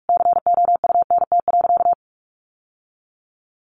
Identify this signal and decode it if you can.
Morse code, heard as a keyed tone, about 44 wpm, 700 Hz. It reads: COUNT.
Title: Sentences from Top 1500 Words 45wpm